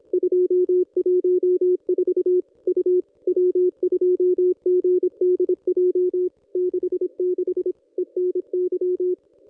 Morse code, heard as a keyed tone, about 26 wpm, 360 Hz. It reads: 214 U W2GDJ 66 ENY
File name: contests / ss-cw-k5tr-op-n5rz -1000